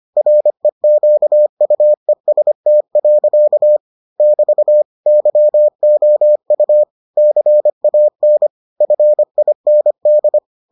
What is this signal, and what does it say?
Morse code, heard as a keyed tone, about 25 wpm, 600 Hz. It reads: REQUEST. <BT> YOU CAN FIND